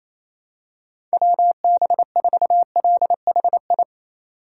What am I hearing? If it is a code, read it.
Morse code, heard as a keyed tone, about 28 wpm, 700 Hz. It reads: WB4LHI
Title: USA Call Signs - 6 Characters - 4x Character Spacing - 28wpm